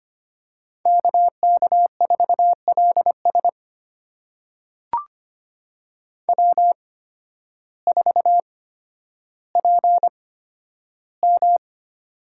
Morse code, heard as a keyed tone, about 25 wpm, spaced as for 12 wpm, 700 Hz. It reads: KK4LS W4PM
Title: USA Call Signs - 8x Character Spacing - 25wpm